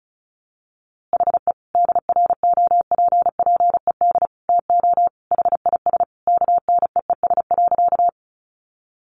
Morse code, heard as a keyed tone, about 34 wpm, 700 Hz. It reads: HE DROPPED TO HIS KNEES.
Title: A Sound Of Thunder 35wpm